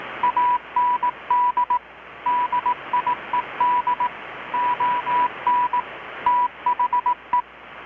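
Morse code, heard as a keyed tone, about 18 wpm, 980 Hz, with radio noise. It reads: AND DIED ON THE
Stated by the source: Morse from KPH's Night of Nights XX maritime radiotelegraphy event, received in Chiapas, Mexico about 3,800 km away on a Kenwood R-600 with a grounded 10 m long-wire antenna.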